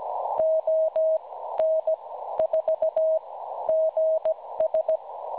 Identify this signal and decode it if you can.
Morse code, heard as a keyed tone, about 16 wpm, spaced as for 12 wpm, 640 Hz. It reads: ON4GS
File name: cwronde / websdr recording .6kHz